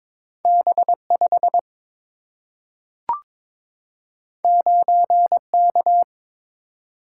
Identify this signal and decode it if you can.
Morse code, heard as a keyed tone, about 22 wpm, 700 Hz. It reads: B 5 9 K
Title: Single-Letter Single-Number 22wpm